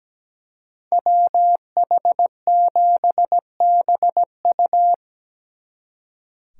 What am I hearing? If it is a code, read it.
Morse code, heard as a keyed tone, about 20 wpm, 700 Hz. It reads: WH7BU
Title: Speed Racing - USA Call Signs 17wpm